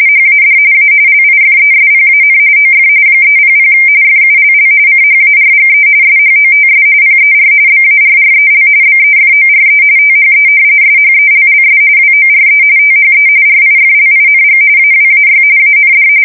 RTTY, about 45.45 baud. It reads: HAD A LIFESPAN OF 300 FULL CHARGE SHOTS PER BARREL BEFORE THE SHELLS WORE THE RIFLING AWAY AND
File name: fsk170 / battleshipnj